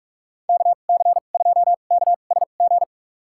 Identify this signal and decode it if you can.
Morse code, heard as a keyed tone, about 41 wpm, spaced as for 31 wpm, 700 Hz. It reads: KC2KSG
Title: USA Call Signs - 2x Character Spacing - 45wpm